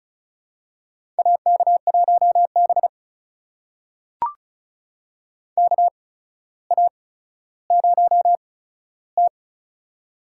Morse code, heard as a keyed tone, about 36 wpm, spaced as for 15 wpm, 700 Hz. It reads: AK1B KA0T
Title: USA Call Signs - 4 Characters - 8x Character Spacing - 35wpm